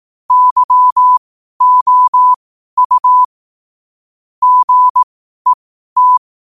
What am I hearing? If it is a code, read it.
Morse code, heard as a keyed tone, about 18 wpm, spaced as for 12 wpm, 1000 Hz. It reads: YOU GET
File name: qotdfiles / 12wpm